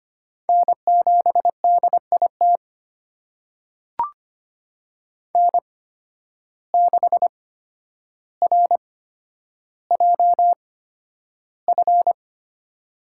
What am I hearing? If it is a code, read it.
Morse code, heard as a keyed tone, about 25 wpm, spaced as for 11 wpm, 700 Hz. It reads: N7DIT N6RJF
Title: USA Call Signs - 5 Characters - 8x Character Spacing - 25wpm